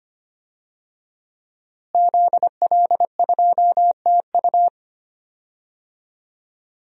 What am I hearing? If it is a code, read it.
Morse code, heard as a keyed tone, about 25 wpm, 700 Hz. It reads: ZL2TU
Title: DXCC-100 Call Signs - 5 Characters - Repeated 3x - 25wpm